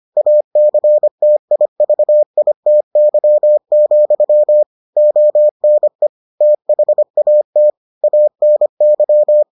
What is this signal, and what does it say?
Morse code, heard as a keyed tone, about 25 wpm, 600 Hz. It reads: ACTIVITY, ONE THAT ANY